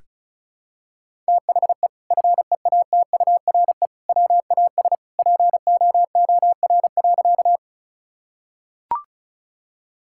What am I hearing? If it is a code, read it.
Morse code, heard as a keyed tone, about 34 wpm, 700 Hz. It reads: THE FEATURE WAS POOR.
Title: Sentences from Top 1000 Words - Review Part 5 - 35wpm